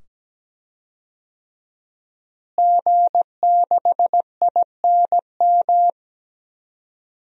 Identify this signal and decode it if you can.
Morse code, heard as a keyed tone, about 17 wpm, 700 Hz. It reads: G6INM
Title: DXCC-100 Call Signs - 5 Characters - Repeated 2x - 17wpm